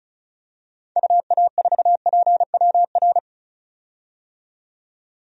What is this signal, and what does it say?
Morse code, heard as a keyed tone, about 36 wpm, 700 Hz. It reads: UA4PWR
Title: DXCC-100 Call Signs - 6 Characters - 35wpm